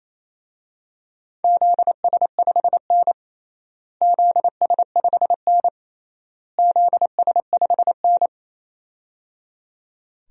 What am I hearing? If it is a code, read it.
Morse code, heard as a keyed tone, about 28 wpm, 700 Hz. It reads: ZS5N ZS5N ZS5N
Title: DXCC-100 Call Signs - 4 Characters - Repeated 3x - 28wpm